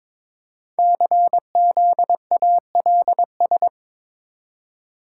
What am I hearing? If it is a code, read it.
Morse code, heard as a keyed tone, about 22 wpm, 700 Hz. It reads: CZALS
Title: Code Groups - 5 Letters - 22wpm